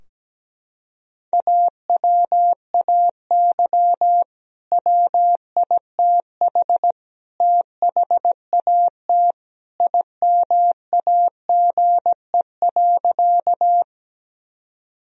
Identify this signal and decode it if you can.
Morse code, heard as a keyed tone, about 17 wpm, 700 Hz. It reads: AWAY WITH THAT IMAGE.
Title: Sentences from Top 1000 Words - Review Part 4 - 17wpm